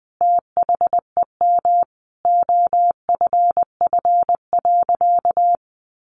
Morse code, heard as a keyed tone, about 20 wpm, 700 Hz. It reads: THEM OFF.
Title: Sentences from Top 1000 Words 20wpm